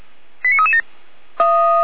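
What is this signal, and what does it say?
Five-tone selective call (ZVEI), heard as quick runs of five tones, tones of about 70 ms. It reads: 89337
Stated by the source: ZVEI five-tone call 89249, received on SDR, logged by feuerwehr-orlamuende fire brigade.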